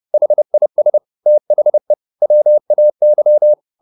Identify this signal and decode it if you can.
Morse code, heard as a keyed tone, about 30 wpm, 600 Hz. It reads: HIS THE WAY